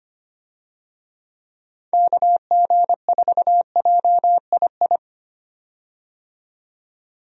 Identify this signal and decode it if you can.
Morse code, heard as a keyed tone, about 25 wpm, 700 Hz. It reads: KG4JII
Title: USA Call Signs - 6 Characters - 25wpm